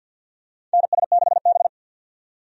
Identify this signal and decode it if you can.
Morse code, heard as a keyed tone, about 49 wpm, 700 Hz. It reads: NS6B
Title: USA Call Signs - 4 Characters - 50wpm